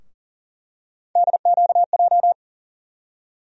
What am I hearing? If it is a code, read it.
Morse code, heard as a keyed tone, about 40 wpm, 700 Hz. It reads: DQJ
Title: USA Call Sign Postfixes - 3 Letters 40wpm